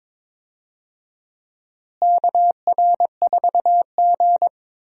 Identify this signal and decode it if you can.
Morse code, heard as a keyed tone, about 22 wpm, 700 Hz. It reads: KR4G
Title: USA Call Signs - 4 Characters - Repeated 3x - 22wpm